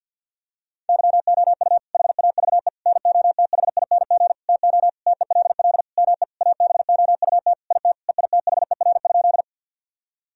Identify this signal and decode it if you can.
Morse code, heard as a keyed tone, about 50 wpm, 700 Hz. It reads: YOU HAVE NOTHING TO TELL ME ABOUT IT EITHER?
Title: Sentences from Top 1000 Words - Review Part 4 - 50wpm